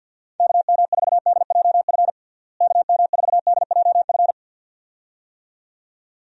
Morse code, heard as a keyed tone, about 53 wpm, 700 Hz. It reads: KM4DJF KM4DJF
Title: USA Call Signs - 6 Characters - Repeated 2x - 50wpm